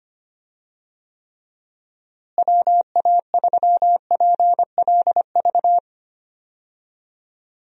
Morse code, heard as a keyed tone, about 25 wpm, 700 Hz. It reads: WA3PLV
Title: USA Call Signs - 6 Characters - Repeated 2x - 25wpm